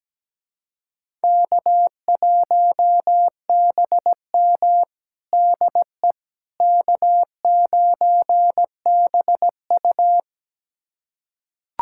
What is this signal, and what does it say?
Morse code, heard as a keyed tone, about 17 wpm, 700 Hz. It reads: K1BM DE K9BU
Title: USA Callsign DE Callsign - 4 Characters - 17wpm